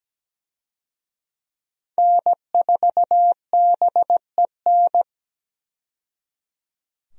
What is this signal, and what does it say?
Morse code, heard as a keyed tone, about 17 wpm, 700 Hz. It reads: N4BEN
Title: USA Call Signs - 5 Characters - 17wpm